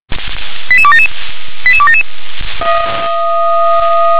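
Five-tone selective call (ZVEI), heard as quick runs of five tones, tones of about 70 ms. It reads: 88277 88277
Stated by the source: ZVEI five-tone call received on SDR.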